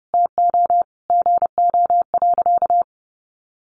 Morse code, heard as a keyed tone, about 30 wpm, 700 Hz. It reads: TO GO.
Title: Sentences from Top 700 Words 30wpm